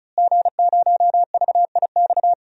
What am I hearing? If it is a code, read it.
Morse code, heard as a keyed tone, about 34 wpm, 700 Hz. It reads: G0VIX
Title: DXCC-100 Call Signs - 5 Characters - Repeated 3x - 35wpm